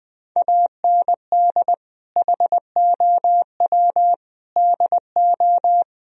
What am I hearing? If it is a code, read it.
Morse code, heard as a keyed tone, about 20 wpm, 700 Hz. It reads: AND HOW DO
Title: Sentences from Top 1000 Words - Review Part 4 - 20wpm